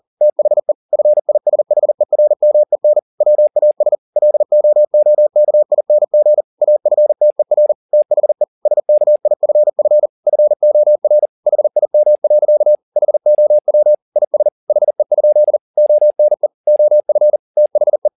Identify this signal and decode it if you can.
Morse code, heard as a keyed tone, about 40 wpm, 600 Hz. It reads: THE FISHERMEN WAS LOOKING AFTER THE SKIFF FOR HIM. HOW IS HE? ONE OF THE
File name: the old man and the sea 2 40wpm